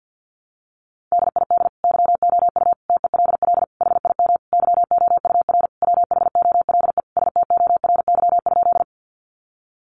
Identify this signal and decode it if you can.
Morse code, heard as a keyed tone, about 50 wpm, 700 Hz. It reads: DID YOU TELL HIM YOUR WHOLE STORY?
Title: Sentences from Top 400 Words 50wpm